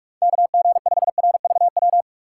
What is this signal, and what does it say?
Morse code, heard as a keyed tone, about 46 wpm, 700 Hz. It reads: KG5RVW
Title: USA Call Signs - 6 Characters - 4x Character Spacing - 45wpm